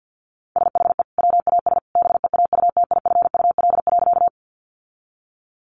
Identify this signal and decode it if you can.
Morse code, heard as a keyed tone, about 50 wpm, 700 Hz. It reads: SHE WAS BEAUTIFUL.
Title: Sentences from Top 900 Words 50wpm